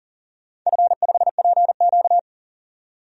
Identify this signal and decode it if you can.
Morse code, heard as a keyed tone, about 40 wpm, 700 Hz. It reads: F5PQ
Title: DXCC-100 Call Signs - 4 Characters - 40wpm